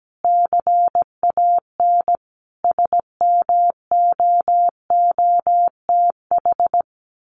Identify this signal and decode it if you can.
Morse code, heard as a keyed tone, about 17 wpm, 700 Hz. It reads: CAN SMOOTH
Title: Sentences from Top 1400 Words 17wpm